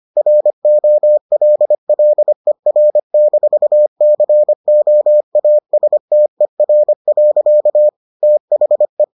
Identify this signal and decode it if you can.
Morse code, heard as a keyed tone, about 25 wpm, 600 Hz. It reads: ROLLER-COASTER. THE